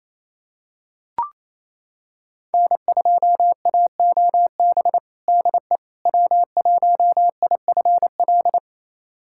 Morse code, heard as a keyed tone, about 28 wpm, 700 Hz. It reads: N2AOB DE W1IFL
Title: USA Callsign DE Callsign - 5 Characters - 28wpm